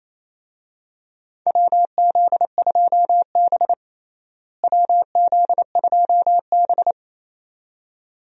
Morse code, heard as a keyed tone, about 28 wpm, 700 Hz. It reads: WZ2B WZ2B
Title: USA Call Signs - 4 Characters - Repeated 2x - 28wpm